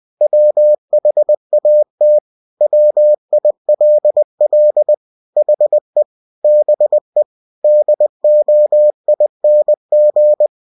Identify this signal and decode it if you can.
Morse code, heard as a keyed tone, about 20 wpm, 600 Hz. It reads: WHAT WILL HE BE DOING
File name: cw-books / winnie-the-pooh / winnie-20wpm-page-0072